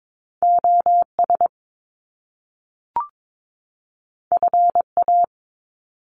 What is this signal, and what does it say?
Morse code, heard as a keyed tone, about 22 wpm, 700 Hz. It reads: OS FA
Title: Top Two Letter Combinations in Words 22wpm